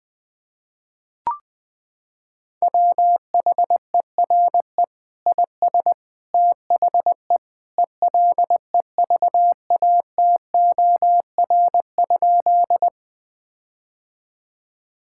Morse code, heard as a keyed tone, about 20 wpm, 700 Hz. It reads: WHERE IS THE ELEVATOR?
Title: Sentences from Top 1600 Words 20wpm